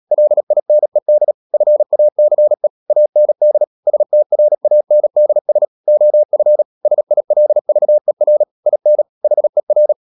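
Morse code, heard as a keyed tone, about 37 wpm, 600 Hz. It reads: LINED FACE AND STRANDS OF SILVER IN HER
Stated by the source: Morse code practice at 35–39 wpm sending English text of Ken Follett's novel Never, from CW med Gnister.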